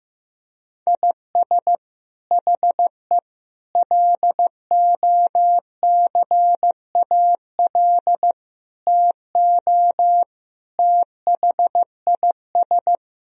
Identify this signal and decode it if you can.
Morse code, heard as a keyed tone, about 15 wpm, 700 Hz. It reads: IS HE LOCAL TO THIS
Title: Sentences from Top 1000 Words - Review Part 3 - 15wpm